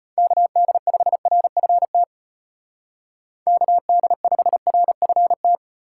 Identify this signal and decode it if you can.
Morse code, heard as a keyed tone, about 31 wpm, 700 Hz. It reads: KD5RFT KD5RFT
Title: Speed Racing - USA 2x3 Call Signs 25wpm